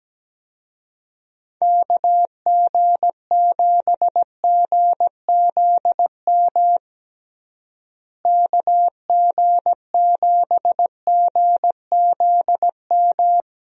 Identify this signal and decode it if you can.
Morse code, heard as a keyed tone, about 17 wpm, 700 Hz. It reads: KG7GZM KG7GZM
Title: USA Call Signs - 6 Characters - Repeated 2x - 17wpm